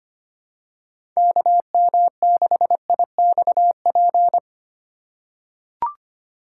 Morse code, heard as a keyed tone, about 25 wpm, 700 Hz. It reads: KM6IXP
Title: USA Call Signs - 6 Characters - 8x Character Spacing - 25wpm